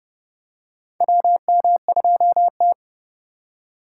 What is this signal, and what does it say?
Morse code, heard as a keyed tone, about 32 wpm, 700 Hz. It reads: WM2T
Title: USA Call Signs - 4 Characters - 2x Character Spacing - 30wpm